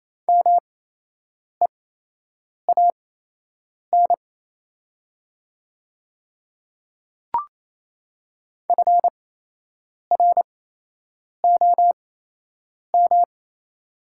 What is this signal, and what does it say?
Morse code, heard as a keyed tone, about 28 wpm, spaced as for 8 wpm, 690 Hz. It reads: MEAN FROM
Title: 8x Character Spacing - Top 200 Words - 28wpm